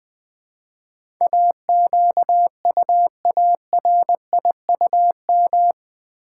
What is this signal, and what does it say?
Morse code, heard as a keyed tone, about 20 wpm, 700 Hz. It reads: AQUARIUM
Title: Easy Words with Hints - 20wpm